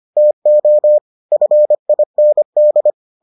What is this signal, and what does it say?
Morse code, heard as a keyed tone, about 25 wpm, 600 Hz. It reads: TO FIND